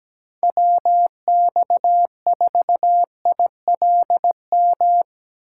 Morse code, heard as a keyed tone, about 19 wpm, 700 Hz. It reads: WX4ILM
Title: Speed Racing - USA Call Signs 17wpm